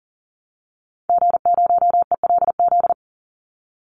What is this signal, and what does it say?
Morse code, heard as a keyed tone, about 40 wpm, 700 Hz. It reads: G0ELZ
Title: Call Signs Repeated 3x - 40wpm